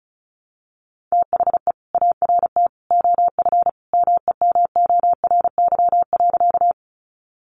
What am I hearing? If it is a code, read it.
Morse code, heard as a keyed tone, about 34 wpm, 700 Hz. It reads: THE ART OF MEMORY.
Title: Sentences from Top 1300 Words 35wpm